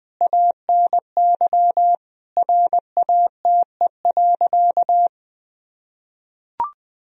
Morse code, heard as a keyed tone, about 19 wpm, 700 Hz. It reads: ANY RATE.
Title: Sentences from Top 1000 Words - Review Part 4 - 20wpm